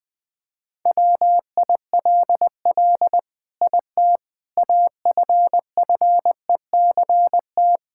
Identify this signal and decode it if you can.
Morse code, heard as a keyed tone, about 21 wpm, 700 Hz. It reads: WILL IT AFFECT
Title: Sentences from Top 1000 Words - Review Part 6 - 20wpm